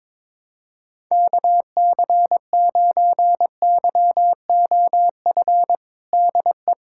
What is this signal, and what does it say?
Morse code, heard as a keyed tone, about 21 wpm, 700 Hz. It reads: KC9YOF DE
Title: USA Callsign DE Callsign 22wpm